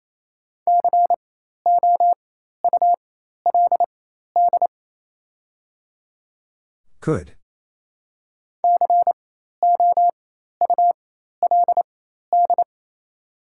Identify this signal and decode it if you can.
Morse code, heard as a keyed tone, about 28 wpm, spaced as for 13 wpm, 690 Hz. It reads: COULD COULD
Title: 4x Character Spacing - Top 200 Words - 28wpm